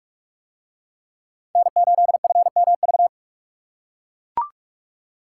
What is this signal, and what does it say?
Morse code, heard as a keyed tone, about 40 wpm, 700 Hz. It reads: N8FMV